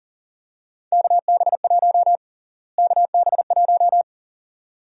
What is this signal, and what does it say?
Morse code, heard as a keyed tone, about 40 wpm, 700 Hz. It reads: KB1 KB1
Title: USA Call Sign Prefixes - Repeated 2x - 40wpm